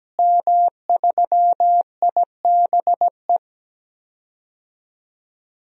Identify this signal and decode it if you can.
Morse code, heard as a keyed tone, about 17 wpm, 700 Hz. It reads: M3IBE
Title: DXCC-100 Call Signs - 5 Characters - Repeated 3x - 17wpm